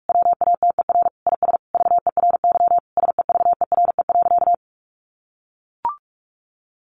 Morse code, heard as a keyed tone, about 44 wpm, 700 Hz. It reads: WATER IS VERY SEVERE.
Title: Sentences from Top 1500 Words 45wpm